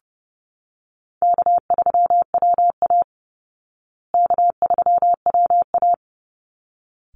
Morse code, heard as a keyed tone, about 30 wpm, 700 Hz. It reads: K3WA K3WA
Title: Easy Call Signs Repeated 2x - 30wpm